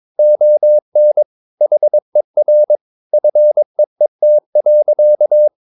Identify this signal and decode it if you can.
Morse code, heard as a keyed tone, about 22 wpm, 600 Hz. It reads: ON HER FEET.
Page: Romanen Never af Ken Follet på engelsk.